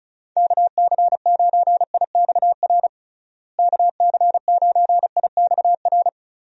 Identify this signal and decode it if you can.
Morse code, heard as a keyed tone, about 34 wpm, 700 Hz. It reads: KC9IXR KC9IXR
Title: USA Call Signs - 6 Characters - Repeated 3x - 35wpm